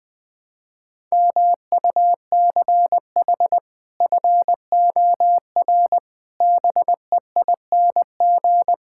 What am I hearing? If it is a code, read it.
Morse code, heard as a keyed tone, about 21 wpm, 700 Hz. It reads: MUCH FOR BEING
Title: Build A Phrase - 5-Word Phrases - 20wpm